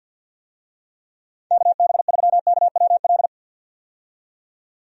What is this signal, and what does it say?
Morse code, heard as a keyed tone, about 53 wpm, 700 Hz. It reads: KB3KWL
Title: USA Call Signs - 6 Characters - 50wpm